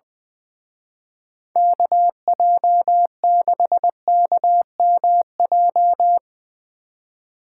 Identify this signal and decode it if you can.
Morse code, heard as a keyed tone, about 22 wpm, 700 Hz. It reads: KJ6KMJ
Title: Speed Racing - USA 2x3 Call Signs 20wpm